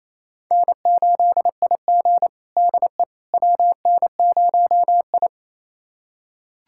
Morse code, heard as a keyed tone, about 27 wpm, 690 Hz. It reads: N8IG DE WN0I
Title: USA Callsign DE Callsign - 4 Characters - 28wpm